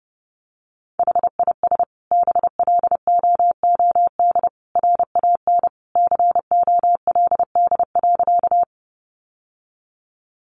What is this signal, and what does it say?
Morse code, heard as a keyed tone, about 30 wpm, 700 Hz. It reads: HIS BLOOD RAN COLD.
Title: Sentences from Top 1100 Words 30wpm